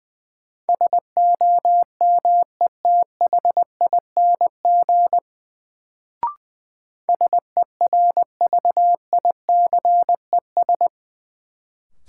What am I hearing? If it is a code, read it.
Morse code, heard as a keyed tone, about 20 wpm, 700 Hz. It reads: SOMETHING SERVICES
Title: Sets of 5 Words - New Word Drill - 20wpm